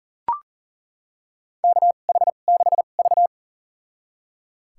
Morse code, heard as a keyed tone, about 39 wpm, spaced as for 28 wpm, 700 Hz. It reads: KH6V
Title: USA Call Signs - 4 Characters - 2x Character Spacing - 40wpm